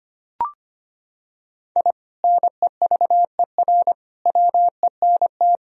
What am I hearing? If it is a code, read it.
Morse code, heard as a keyed tone, about 25 wpm, 700 Hz. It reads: I NEVER WENT
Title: Sentences from Top 1000 Words - Review Part 3 - 25wpm